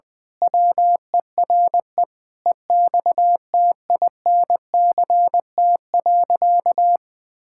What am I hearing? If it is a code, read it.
Morse code, heard as a keyed tone, about 20 wpm, 700 Hz. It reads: WERE EXTINCT.
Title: Pangram Challenge v2 - 20wpm